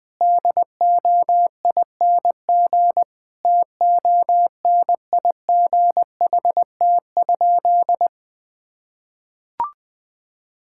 Morse code, heard as a keyed tone, about 20 wpm, 700 Hz. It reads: DOING TONIGHT?
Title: Sentences with 4-Word Phrases - Spoken Beginning - 20wpm